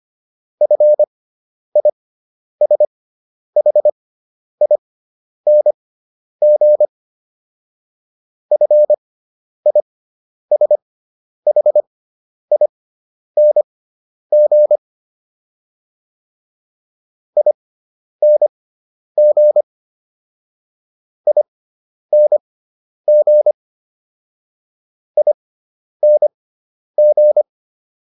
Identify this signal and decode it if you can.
Morse code, heard as a keyed tone, about 25 wpm, spaced as for 10 wpm, 600 Hz. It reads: FISHING FISHING ING ING ING